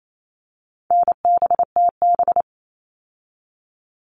Morse code, heard as a keyed tone, about 28 wpm, 700 Hz. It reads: NBTB
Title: All Cut Numbers - 4-Digit 28wpm